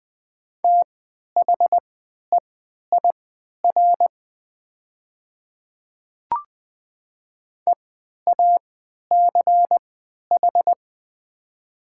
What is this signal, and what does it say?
Morse code, heard as a keyed tone, about 20 wpm, spaced as for 11 wpm, 700 Hz. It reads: THEIR EACH
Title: 3x Character Spacing - Top 100 Words - 20wpm